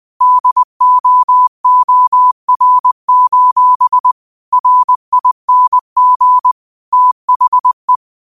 Morse code, heard as a keyed tone, about 20 wpm, 1000 Hz. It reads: DOOR: RING THE